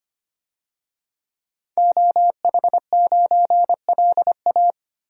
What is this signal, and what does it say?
Morse code, heard as a keyed tone, about 25 wpm, 700 Hz. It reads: OH9LA
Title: DXCC-100 Call Signs - 5 Characters - 25wpm